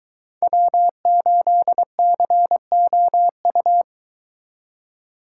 Morse code, heard as a keyed tone, about 22 wpm, 700 Hz. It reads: W8COU